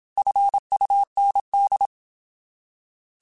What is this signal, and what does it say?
Morse code, heard as a keyed tone, about 27 wpm, 800 Hz. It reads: FUND
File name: Alphabetical / Alphabetical mp3s / Fs @25